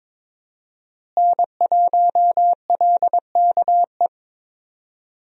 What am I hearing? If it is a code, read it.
Morse code, heard as a keyed tone, about 22 wpm, 700 Hz. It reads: N1LKE
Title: USA Call Signs - 5 Characters - Repeated 2x - 22wpm